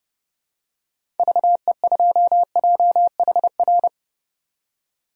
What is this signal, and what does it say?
Morse code, heard as a keyed tone, about 30 wpm, 700 Hz. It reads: VE2JHR
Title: DXCC-100 Call Signs - 6 Characters - 30wpm